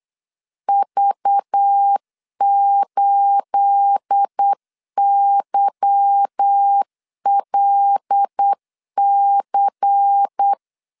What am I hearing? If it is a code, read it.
Morse code, heard as a keyed tone, about 8 wpm, 800 Hz. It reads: V8YLC
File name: LEZIONI / 9 286 8wpm 3425